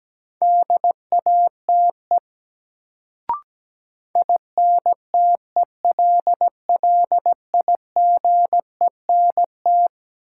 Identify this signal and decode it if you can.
Morse code, heard as a keyed tone, about 17 wpm, 700 Hz. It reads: DATE INTELLIGENT
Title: Sentences from Top 1000 Words - Review Part 7 - New Word Drill - 17wpm